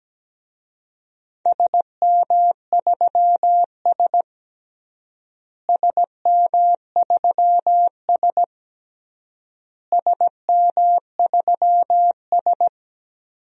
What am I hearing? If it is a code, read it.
Morse code, heard as a keyed tone, about 17 wpm, 700 Hz. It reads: SM3S SM3S SM3S
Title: DXCC-100 Call Signs - 4 Characters - Repeated 3x - 17wpm